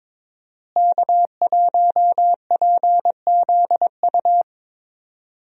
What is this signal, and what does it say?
Morse code, heard as a keyed tone, about 22 wpm, 700 Hz. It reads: K1PZU
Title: USA Call Signs - 5 Characters - Repeated 2x - 22wpm